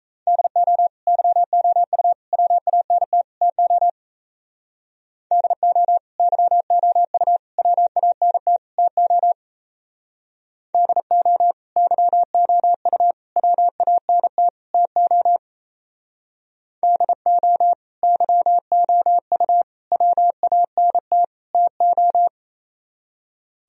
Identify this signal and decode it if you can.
Morse code, heard as a keyed tone, about 34 wpm, 700 Hz. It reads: DO YOU WANT TO DO YOU WANT TO DO YOU WANT TO DO YOU WANT TO
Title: Speed-Racing - Sets of 4 Words 28wpm